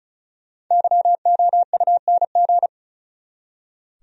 Morse code, heard as a keyed tone, about 44 wpm, 700 Hz. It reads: YOUNG